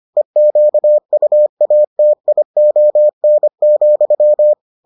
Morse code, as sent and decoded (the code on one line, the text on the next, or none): . --.- ..- .- - .. --- -. --..--
EQUATION,